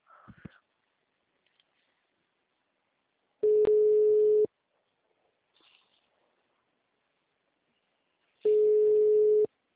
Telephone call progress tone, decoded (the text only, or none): ringback tone